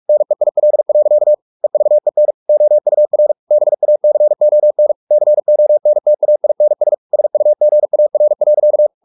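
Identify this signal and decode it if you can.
Morse code, heard as a keyed tone, about 44 wpm, 600 Hz. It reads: NEIL. EVEN OUR BACON CONTAINS SUGAR.
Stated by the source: Morse practice at 45 wpm, the novel Never by Ken Follett in English, from CW med Gnister.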